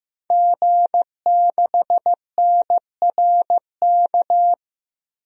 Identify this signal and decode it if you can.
Morse code, heard as a keyed tone, about 16 wpm, 700 Hz. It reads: G6NRK
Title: DXCC-100 Call Signs - 5 Characters - 15wpm